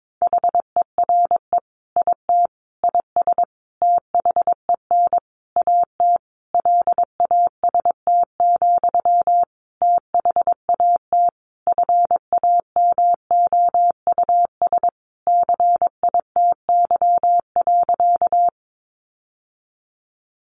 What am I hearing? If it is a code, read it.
Morse code, heard as a keyed tone, about 22 wpm, 700 Hz. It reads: HERE IT IS THEN AT LAST, THAT FAMOUS CITY.
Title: Sentences from Top 1500 Words 22wpm